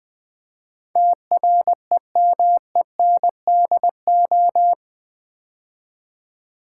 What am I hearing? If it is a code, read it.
Morse code, heard as a keyed tone, about 20 wpm, 700 Hz. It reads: TREMENDO
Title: Build A Word - 4 Letters At A Time - 10-Letter Words - 20wpm